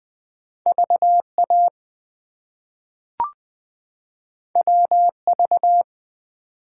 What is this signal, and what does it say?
Morse code, heard as a keyed tone, about 20 wpm, 700 Hz. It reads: VA WV
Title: US States and Canadian Provinces 20wpm